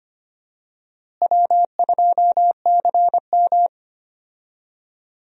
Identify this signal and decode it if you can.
Morse code, heard as a keyed tone, about 25 wpm, 700 Hz. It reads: W2CM
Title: USA Call Signs - 4 Characters - Repeated 2x - 25wpm